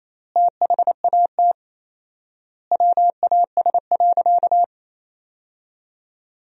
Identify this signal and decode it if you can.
Morse code, heard as a keyed tone, about 28 wpm, 700 Hz. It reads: THAT WAS.
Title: 4x Word-Spacing - Sentences from Top 300 Words - 28wpm